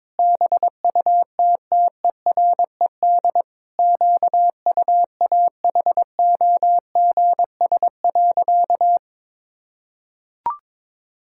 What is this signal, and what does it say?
Morse code, heard as a keyed tone, about 22 wpm, 700 Hz. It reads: BUTTERED QUAHOGS.
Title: Pangram Challenge v2 - 22wpm